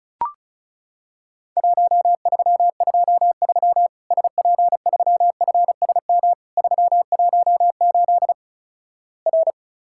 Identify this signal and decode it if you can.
Morse code, heard as a keyed tone, about 36 wpm, 700 Hz. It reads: 1323 SP3FSM 318 R
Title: Worked All Europe DX Contest - QTC Exchange - 35wpm